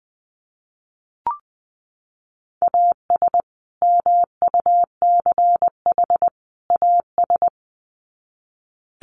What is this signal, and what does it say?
Morse code, heard as a keyed tone, about 20 wpm, 700 Hz. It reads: AS MUCH AS
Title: Sets of 3 Words 20wpm